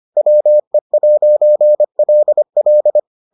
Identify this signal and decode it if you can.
Morse code, heard as a keyed tone, about 25 wpm, 600 Hz. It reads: WE'LL